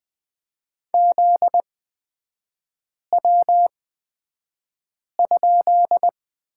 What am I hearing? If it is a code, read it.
Morse code, heard as a keyed tone, about 20 wpm, spaced as for 5 wpm, 700 Hz. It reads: ZW?
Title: 243 Lesson 39e - Letter z - Three Characters 20wpm